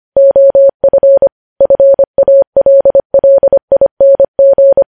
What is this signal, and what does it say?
Morse code, heard as a keyed tone, about 25 wpm, 550 Hz. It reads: OF FALLING
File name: podcast / News headlines-1x-25x25